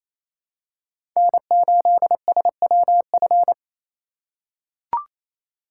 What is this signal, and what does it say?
Morse code, heard as a keyed tone, about 28 wpm, 700 Hz. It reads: N8SWF
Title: USA Call Signs - 2x Character Spacing - 28wpm